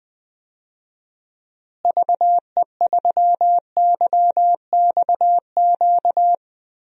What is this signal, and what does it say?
Morse code, heard as a keyed tone, about 20 wpm, 700 Hz. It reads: VE3YXQ
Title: DXCC-100 Call Signs - 6 Characters - Repeated 3x - 20wpm